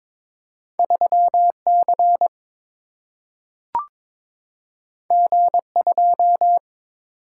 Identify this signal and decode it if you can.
Morse code, heard as a keyed tone, about 22 wpm, 700 Hz. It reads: 3C G2
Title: Single-Letter Single-Number 22wpm